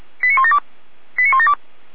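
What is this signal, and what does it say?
Five-tone selective call (ZVEI), heard as quick runs of five tones, tones of about 70 ms. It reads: 89172 89172
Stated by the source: ZVEI five-tone call 89338 received by SDR, channel 455.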